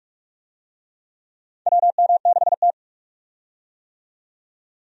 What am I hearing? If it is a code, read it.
Morse code, heard as a keyed tone, about 46 wpm, 700 Hz. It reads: WM6T
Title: USA Call Signs - 4 Characters - 45wpm